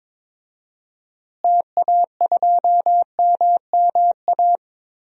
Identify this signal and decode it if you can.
Morse code, heard as a keyed tone, about 22 wpm, 700 Hz. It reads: TA2MMA
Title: DXCC-100 Call Signs - 6 Characters - Repeated 2x - 22wpm